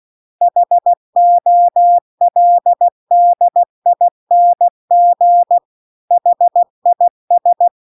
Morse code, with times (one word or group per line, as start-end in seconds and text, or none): 0.4-5.6 HOLDING
6.1-7.7 HIS